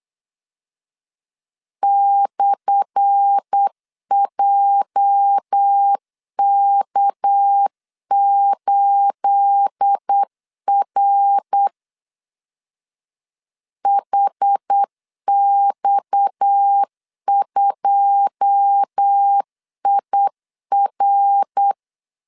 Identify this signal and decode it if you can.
Morse code, heard as a keyed tone, about 8 wpm, 800 Hz. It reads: /JK8R HX2IR
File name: LEZIONI / 10 SIGNS / 10A SLASH 8wpm 3452